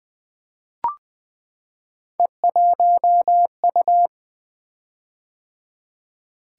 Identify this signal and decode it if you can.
Morse code, heard as a keyed tone, about 20 wpm, 700 Hz. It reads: E1U
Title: Three Letter-Number Combinations - 20wpm